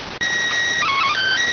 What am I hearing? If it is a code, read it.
SSTV (slow-scan television), PD 290.